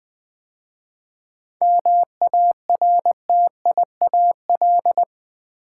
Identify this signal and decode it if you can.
Morse code, heard as a keyed tone, about 20 wpm, 700 Hz. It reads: MARTIAL
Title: 7 Letter Words - First 3 Letters Given - 20wpm